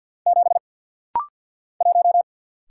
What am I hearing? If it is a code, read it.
Morse code, heard as a keyed tone, about 49 wpm, 700 Hz. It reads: 7 1
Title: Single Letter-Number - Rapid-Fire 50wpm